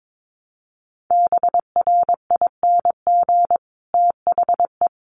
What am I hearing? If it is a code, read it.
Morse code, heard as a keyed tone, about 22 wpm, 700 Hz. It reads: BRING THE
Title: Sentences from Top 1300 Words 22wpm